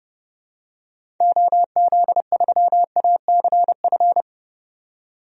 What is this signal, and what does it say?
Morse code, heard as a keyed tone, about 30 wpm, 700 Hz. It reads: OZ3ACF
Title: DXCC-100 Call Signs - 6 Characters - 30wpm